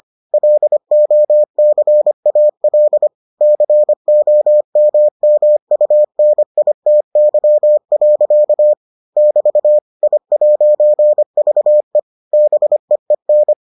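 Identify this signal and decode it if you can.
Morse code, heard as a keyed tone, about 25 wpm, 600 Hz. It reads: LOCAL COMMUNITY. <BT> I'VE BEEN
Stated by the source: Morse edition of the Foundations of Amateur Radio episodes.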